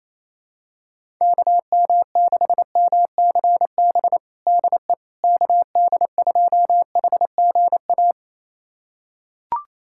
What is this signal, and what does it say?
Morse code, heard as a keyed tone, about 28 wpm, 700 Hz. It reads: KM6MCB DE KD2HGA
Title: USA Callsign DE Callsign 28wpm